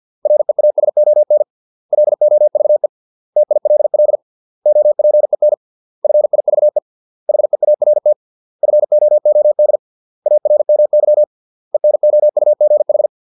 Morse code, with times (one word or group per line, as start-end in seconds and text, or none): 0.3-1.4 REASON
1.9-2.9 LOVE
3.4-4.2 TILL
4.7-5.5 OPEN
6.0-6.8 FIVE
7.3-8.1 HEART
8.6-9.8 FOOD
10.3-11.2 ARMY
11.7-13.1 ENOUGH